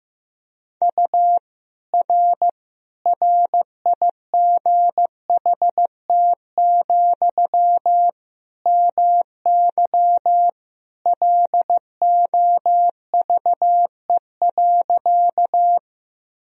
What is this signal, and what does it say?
Morse code, heard as a keyed tone, about 15 wpm, 700 Hz. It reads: U R RIGHT, MY LOVE.
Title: Sentences from Top 400 Words and Abbreviations 15wpm